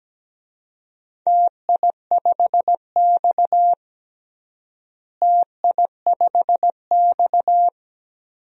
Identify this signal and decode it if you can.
Morse code, heard as a keyed tone, about 17 wpm, 700 Hz. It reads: TI5X TI5X
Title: DXCC-100 Call Signs - 4 Characters - Repeated 3x - 17wpm